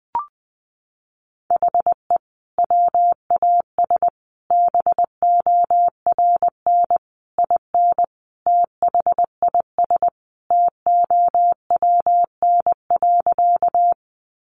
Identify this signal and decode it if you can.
Morse code, heard as a keyed tone, about 20 wpm, 700 Hz. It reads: HE WAS BORN IN THIS TOWN.
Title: Sentences from Top 1000 Words 20wpm